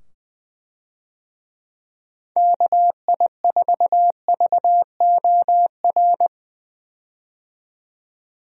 Morse code, heard as a keyed tone, about 20 wpm, 700 Hz. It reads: KI4VOR
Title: USA Call Signs - 6 Characters - 20wpm